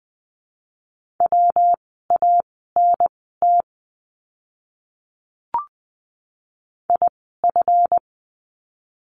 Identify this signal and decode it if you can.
Morse code, heard as a keyed tone, about 20 wpm, spaced as for 15 wpm, 700 Hz. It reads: WANT IF
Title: 230 Lesson 37f - Letter x - Words 20wpm